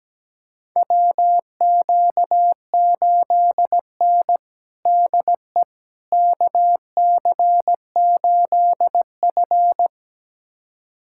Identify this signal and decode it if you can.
Morse code, heard as a keyed tone, about 17 wpm, 700 Hz. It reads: WQ8N DE KC8F
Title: USA Callsign DE Callsign - 4 Characters - 17wpm